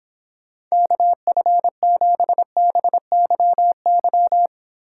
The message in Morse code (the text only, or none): KF7BYY